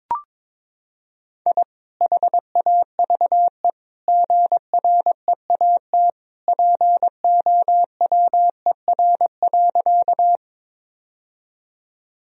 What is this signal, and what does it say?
Morse code, heard as a keyed tone, about 22 wpm, 700 Hz. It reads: I HAVE GREAT POWER.
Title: Sentences from Top 1000 Words - Review Part 2 - 22wpm